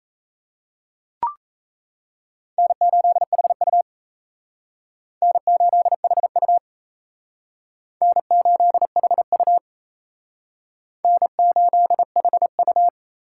Morse code, heard as a keyed tone, about 35 wpm, 700 Hz. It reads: N8HU N8HU N8HU N8HU
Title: Speed Racing - USA 1x2 Call Signs 28wpm